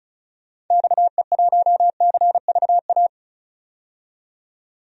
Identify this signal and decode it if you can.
Morse code, heard as a keyed tone, about 36 wpm, 700 Hz. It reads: XE1CVA
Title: DXCC-100 Call Signs - 6 Characters - 35wpm